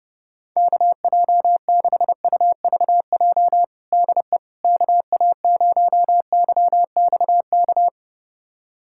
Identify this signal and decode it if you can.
Morse code, heard as a keyed tone, about 30 wpm, 700 Hz. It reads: KJ6UVJ DE KA0YXK
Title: USA Callsign DE Callsign - 6 Characters - 30wpm